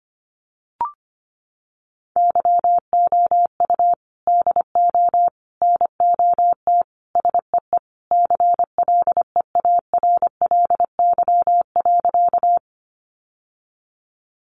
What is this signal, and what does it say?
Morse code, heard as a keyed tone, about 25 wpm, 700 Hz. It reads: YOU DO NOT SEE CLEARLY.
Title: Sentences from Top 1300 Words 25wpm